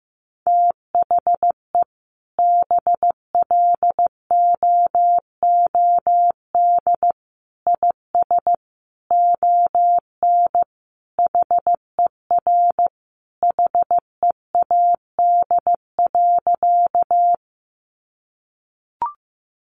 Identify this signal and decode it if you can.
Morse code, heard as a keyed tone, about 15 wpm, 700 Hz. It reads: THE BLOOD IS ON HER HEAD.
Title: Sentences from Top 1100 Words 15wpm